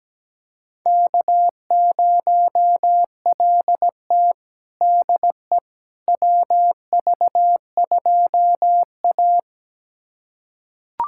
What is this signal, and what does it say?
Morse code, heard as a keyed tone, about 17 wpm, 700 Hz. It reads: K0LT DE WV2A
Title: USA Callsign DE Callsign - 4 Characters - 17wpm